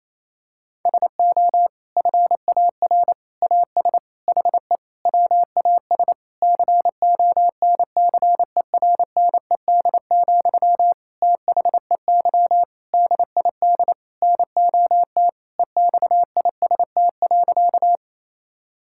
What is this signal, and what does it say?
Morse code, heard as a keyed tone, about 28 wpm, 700 Hz. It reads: SO FAR AS HE WAS CONCERNED, THEY DID NOT EXIST.
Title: Sentences from Top 1000 Words - Review Part 6 - 28wpm